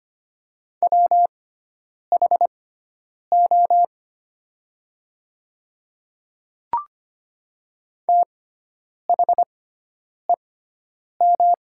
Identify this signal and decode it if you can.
Morse code, heard as a keyed tone, about 25 wpm, spaced as for 9 wpm, 700 Hz. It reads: WHO THEM
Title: 6x Character Spacing - Top 200 Words - 25wpm